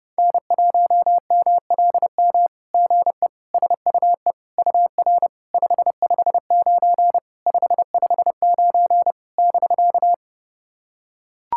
Morse code, heard as a keyed tone, about 30 wpm, 660 Hz. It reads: N1MLM GE SUE UR 559 559 <BK>
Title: SOTA QSO Practice 30wpm